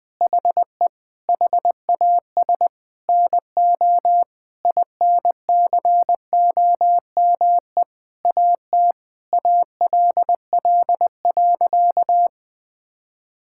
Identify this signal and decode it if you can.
Morse code, heard as a keyed tone, about 20 wpm, 700 Hz. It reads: HE HAS NO INCOME AT ALL.
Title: Sentences from Top 1000 Words - Review Part 5 - 20wpm